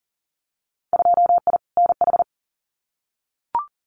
Morse code, heard as a keyed tone, about 42 wpm, 700 Hz. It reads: 2I NH
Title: Winter Field Day Exchanges Encore 40wpm